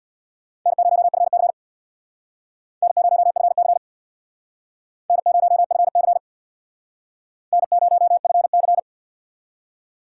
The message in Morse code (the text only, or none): N0FC N0FC N0FC N0FC